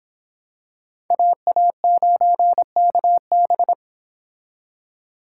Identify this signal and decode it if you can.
Morse code, heard as a keyed tone, about 25 wpm, 700 Hz. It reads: AA9KB